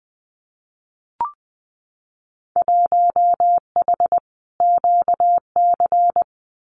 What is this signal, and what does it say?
Morse code, heard as a keyed tone, about 20 wpm, 700 Hz. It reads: 1H QC
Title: Winter Field Day Exchanges 20wpm